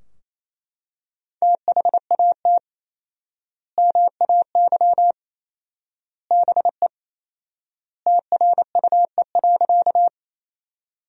Morse code, heard as a keyed tone, about 28 wpm, 700 Hz. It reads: THAT MAY BE TRUE.
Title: 4x Word-Spacing - Sentences from Top 400 Words - 28wpm